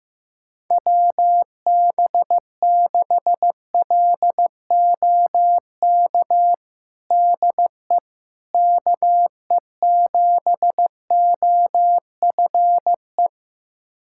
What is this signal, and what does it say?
Morse code, heard as a keyed tone, about 15 wpm, 700 Hz. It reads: WB6LOK DE KE7OFE
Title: USA Callsign DE Callsign - 6 Characters - 15wpm